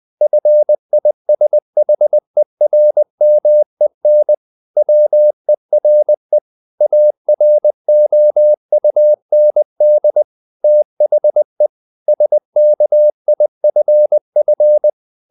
Morse code, heard as a keyed tone, about 20 wpm, 600 Hz. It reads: FISHERMEN WERE AROUND THE SKIFF
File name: the old man and the sea part 2 20wpm